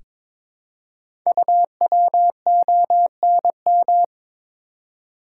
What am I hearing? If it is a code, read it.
Morse code, heard as a keyed tone, about 22 wpm, 700 Hz. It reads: UWONM
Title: Code Groups - 5 Letters - 22wpm